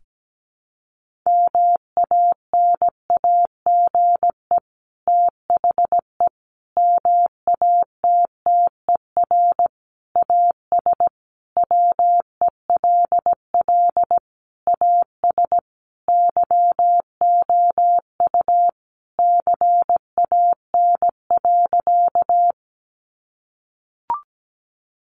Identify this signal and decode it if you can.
Morse code, heard as a keyed tone, about 17 wpm, 700 Hz. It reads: MANAGE THE MATTER AS WELL AS YOU CAN.
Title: Sentences from Top 1400 Words 17wpm